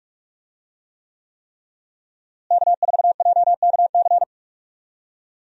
Morse code, heard as a keyed tone, about 43 wpm, 700 Hz. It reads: K4JKC
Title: USA Call Signs - 5 Characters - Repeated 3x - 45wpm